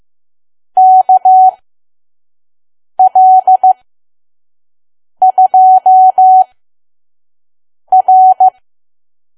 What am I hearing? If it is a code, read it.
Morse code, heard as a keyed tone, about 15 wpm, spaced as for 5 wpm, 750 Hz. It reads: KL2R
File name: Archive / 5 WPM files / PM